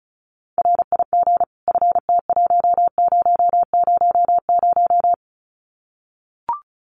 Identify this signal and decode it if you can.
Morse code, heard as a keyed tone, about 34 wpm, 700 Hz. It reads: RIG FT1000
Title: Common QSO Phrases 35wpm